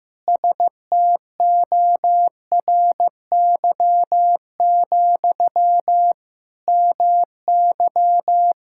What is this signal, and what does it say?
Morse code, heard as a keyed tone, about 15 wpm, 700 Hz. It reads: STORY, MY
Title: Sentences from Top 1000 Words - Review Part 6 - 15wpm